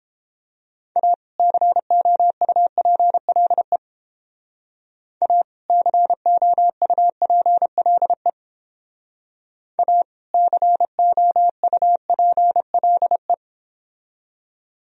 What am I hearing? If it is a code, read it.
Morse code, heard as a keyed tone, about 29 wpm, 700 Hz. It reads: A COUPLE A COUPLE A COUPLE